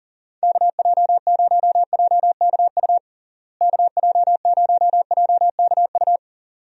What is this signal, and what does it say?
Morse code, heard as a keyed tone, about 40 wpm, 700 Hz. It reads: KJ0JKU KJ0JKU
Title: USA Call Signs - 6 Characters - Repeated 2x - 40wpm